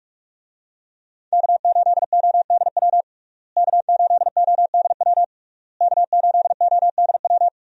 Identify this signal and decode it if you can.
Morse code, heard as a keyed tone, about 46 wpm, 700 Hz. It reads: K8ODW K8ODW K8ODW
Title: USA Call Signs - 5 Characters - Repeated 3x - 45wpm